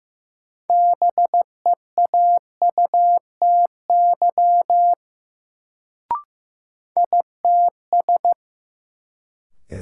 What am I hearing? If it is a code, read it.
Morse code, heard as a keyed tone, about 15 wpm, 700 Hz. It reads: BEAUTY ITS
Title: Sentences from Top 600 Words - New Word Drill - 15wpm